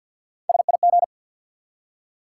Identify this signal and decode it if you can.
Morse code, heard as a keyed tone, about 48 wpm, 700 Hz. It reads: SIG
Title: Top 90 QSO Elements 50wpm